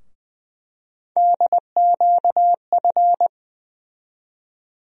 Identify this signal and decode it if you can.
Morse code, heard as a keyed tone, about 21 wpm, 700 Hz. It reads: DQF
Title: USA Call Sign Postfixes - 3 Letters 20wpm